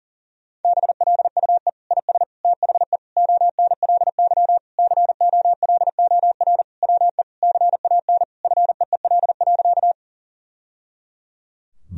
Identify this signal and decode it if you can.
Morse code, heard as a keyed tone, about 40 wpm, 700 Hz. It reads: BLUE IS THE ONLY COLOR WE CAN FEEL.
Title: Sentences from Top 1000 Words - Review Part 3 - 40wpm